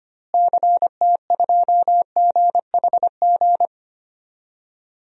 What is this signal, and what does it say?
Morse code, heard as a keyed tone, about 25 wpm, 700 Hz. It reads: CT2GHG
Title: DXCC-100 Call Signs - 6 Characters - Repeated 3x - 25wpm